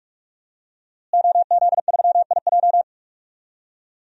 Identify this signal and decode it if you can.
Morse code, heard as a keyed tone, about 46 wpm, 700 Hz. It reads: OZ3IJ
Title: DXCC-100 Call Signs - 5 Characters - Repeated 3x - 45wpm